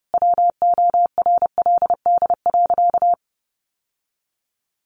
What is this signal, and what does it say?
Morse code, heard as a keyed tone, about 30 wpm, 700 Hz. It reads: WORLD.
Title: Sentences from Top 700 Words 30wpm